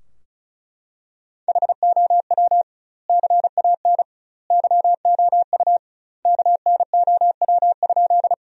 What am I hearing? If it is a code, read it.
Morse code, heard as a keyed tone, about 36 wpm, 700 Hz. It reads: HOW CAN YOU KNOW?
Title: 2x Word-Spacing - Sentences from Top 100 Words - 35wpm